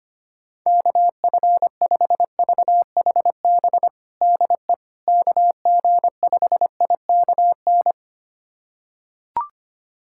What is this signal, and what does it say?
Morse code, heard as a keyed tone, about 25 wpm, 700 Hz. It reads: KF5VHB DE KG5IKN
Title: USA Callsign DE Callsign - 6 Characters - 25wpm